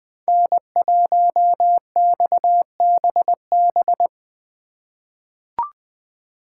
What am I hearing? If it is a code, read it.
Morse code, heard as a keyed tone, about 20 wpm, 700 Hz. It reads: N1XBB
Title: USA Call Signs - 5 Characters - 4x Character Spacing - 20wpm